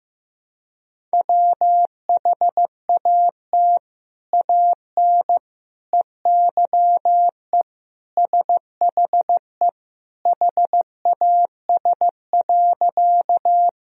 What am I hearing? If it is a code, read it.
Morse code, heard as a keyed tone, about 15 wpm, 700 Hz. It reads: WHAT AN EYE SHE HAS.
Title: Sentences from Top 1000 Words - Review Part 1 - 15wpm